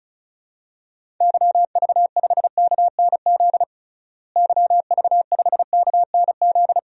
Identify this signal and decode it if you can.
Morse code, heard as a keyed tone, about 34 wpm, 700 Hz. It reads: YV5KNZ YV5KNZ
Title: DXCC-100 Call Signs - 6 Characters - Repeated 3x - 35wpm